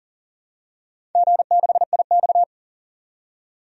Morse code, heard as a keyed tone, about 37 wpm, 700 Hz. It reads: G6IX
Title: DXCC-100 Call Signs - 4 Characters - 40wpm